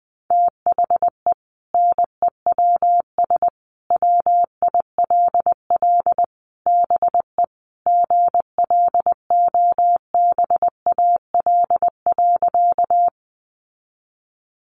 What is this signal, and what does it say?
Morse code, heard as a keyed tone, about 20 wpm, 700 Hz. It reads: THE NEWS WILL BE GLOBAL.
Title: Sentences from Top 1200 Words 20wpm